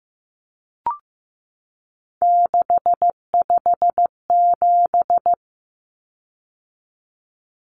Morse code, heard as a keyed tone, about 15 wpm, 700 Hz. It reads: 657
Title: Common Cut Numbers - 3-Digit 15wpm